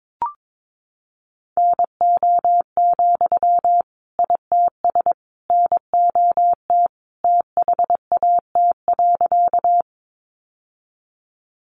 Morse code, heard as a keyed tone, about 22 wpm, 700 Hz. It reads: NO, ITS NOT THAT.
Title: Sentences from Top 600 Words 22wpm